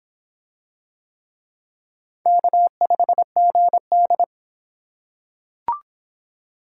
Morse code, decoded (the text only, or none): K5GD